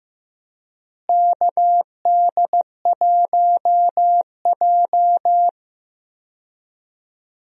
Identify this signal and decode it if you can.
Morse code, heard as a keyed tone, about 15 wpm, 700 Hz. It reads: KD1J
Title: USA Call Signs - 4 Characters - Repeated 2x - 15wpm